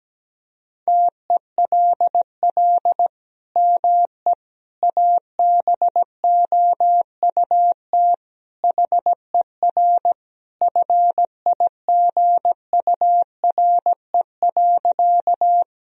Morse code, heard as a keyed tone, about 17 wpm, 700 Hz. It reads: TELL ME ABOUT HER FIGURE.
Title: Sentences from Top 1000 Words - Review Part 2 - 17wpm